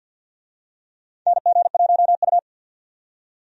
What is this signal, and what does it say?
Morse code, heard as a keyed tone, about 49 wpm, 700 Hz. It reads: NG1U
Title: USA Call Signs - 4 Characters - 50wpm